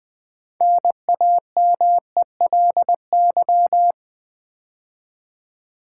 Morse code, heard as a keyed tone, about 20 wpm, 700 Hz. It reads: NAMELY
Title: 6 Letter Words - First 3 Letters Given - 20wpm